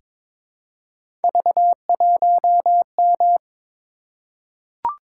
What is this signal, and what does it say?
Morse code, heard as a keyed tone, about 22 wpm, 700 Hz. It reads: V1M
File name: practice / 22wpm / Three Letter-Number Combinations - 22wpm